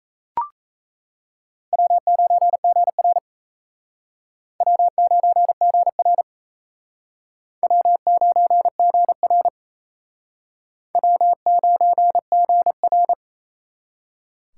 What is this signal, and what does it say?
Morse code, heard as a keyed tone, about 34 wpm, 700 Hz. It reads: W9GR W9GR W9GR W9GR
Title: Speed Racing - USA 1x2 Call Signs 28wpm